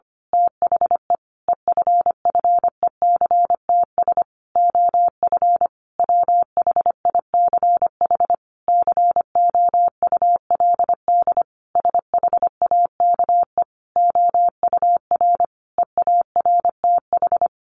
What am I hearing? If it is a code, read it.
Morse code, heard as a keyed tone, about 25 wpm, 700 Hz. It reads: THE EFFECTS OF WHICH COULD SHAKE OUR EARTH
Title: A Sound Of Thunder 25wpm